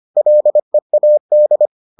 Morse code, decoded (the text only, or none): LEAD